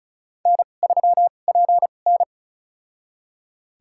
Morse code, heard as a keyed tone, about 36 wpm, spaced as for 25 wpm, 700 Hz. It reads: N3PN
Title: USA Call Signs - 4 Characters - 2x Character Spacing - 35wpm